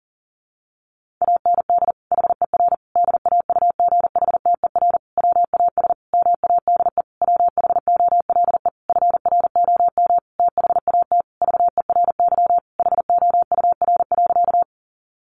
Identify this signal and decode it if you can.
Morse code, heard as a keyed tone, about 40 wpm, 700 Hz. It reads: AND HER DAUGHTER WAS MADE WHOLE FROM THAT VERY HOUR.
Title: Sentences from Top 1500 Words 40wpm